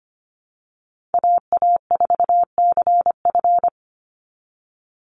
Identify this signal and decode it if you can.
Morse code, heard as a keyed tone, about 25 wpm, 700 Hz. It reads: AA4CF
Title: Call Signs Repeated 2x - 25wpm